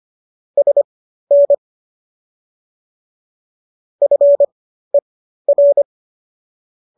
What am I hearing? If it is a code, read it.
Morse code, heard as a keyed tone, about 25 wpm, spaced as for 13 wpm, 570 Hz. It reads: SN FER